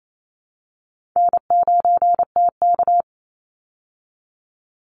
Morse code, heard as a keyed tone, about 28 wpm, 700 Hz. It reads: N9TK
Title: Easy Call Signs Repeated 3x - 28wpm